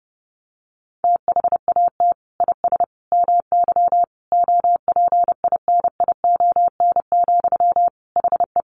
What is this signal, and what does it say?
Morse code, heard as a keyed tone, about 28 wpm, 700 Hz. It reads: THAT IS MY OPINION, HE
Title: Sentences from Top 1200 Words 30wpm